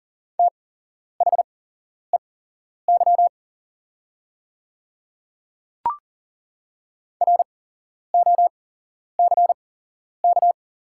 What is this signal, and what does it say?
Morse code, heard as a keyed tone, about 40 wpm, spaced as for 11 wpm, 690 Hz. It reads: THEY ROCK
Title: 8x Character Spacing - Top 500 Words - 40wpm